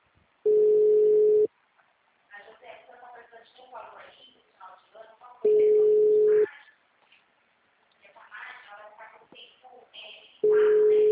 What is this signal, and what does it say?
Ringback tone (Europe, 425 Hz), with speech.
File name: out- -2004- - - .45389